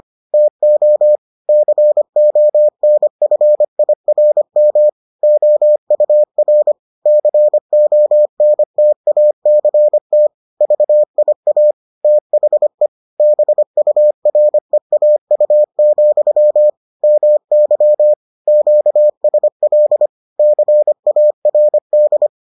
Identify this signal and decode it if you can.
Morse code, heard as a keyed tone, about 25 wpm, 600 Hz. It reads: TO CONFIRM OUR CONTACT VIA THE BUREAU, MY QSL CARD